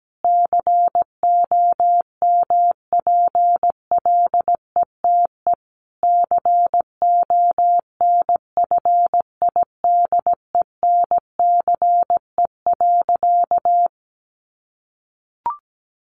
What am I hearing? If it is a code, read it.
Morse code, heard as a keyed tone, about 17 wpm, 700 Hz. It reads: COMPLETE CONFIDENCE.
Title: Sentences from Top 1500 Words 17wpm